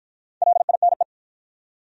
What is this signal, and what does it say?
Morse code, heard as a keyed tone, about 53 wpm, 700 Hz. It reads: LINE